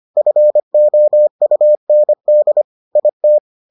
Morse code, heard as a keyed tone, about 25 wpm, 600 Hz. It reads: FOUND IT